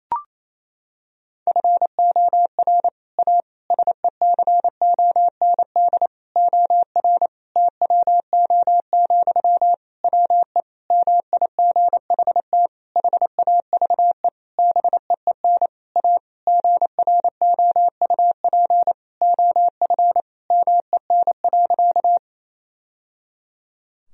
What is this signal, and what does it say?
Morse code, heard as a keyed tone, about 28 wpm, 700 Hz. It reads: FOR A SECOND OR TWO, WE MIGHT HAVE BEEN A GROUP OF MEN.
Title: Sentences from Top 1000 Words - Review Part 1 - 28wpm